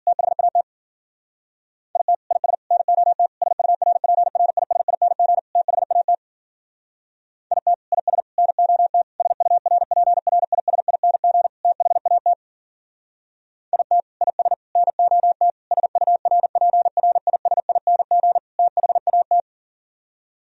Morse code, heard as a keyed tone, about 46 wpm, 700 Hz. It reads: THAT IT IS NOT SURPRISING THAT IT IS NOT SURPRISING THAT IT IS NOT SURPRISING THAT
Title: Speed-Racing - Sets of 5 Words 40wpm